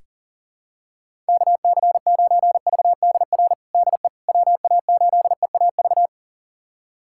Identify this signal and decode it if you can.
Morse code, heard as a keyed tone, about 40 wpm, 700 Hz. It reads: KC9VDR DE WA8EAV
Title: USA Callsign DE Callsign 40wpm